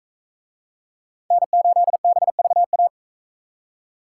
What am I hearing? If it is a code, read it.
Morse code, heard as a keyed tone, about 40 wpm, 700 Hz. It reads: N8BVA